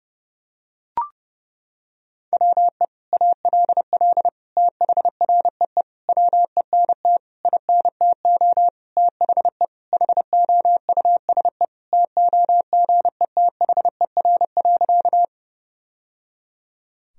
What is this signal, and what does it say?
Morse code, heard as a keyed tone, about 32 wpm, 700 Hz. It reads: WE ALL THREE WENT INTO THE HOUSE TOGETHER.
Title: Sentences from Top 1000 Words - Review Part 1 - 30wpm